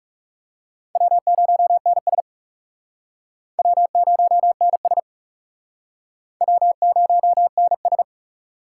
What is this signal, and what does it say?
Morse code, heard as a keyed tone, about 40 wpm, 700 Hz. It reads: W0NS W0NS W0NS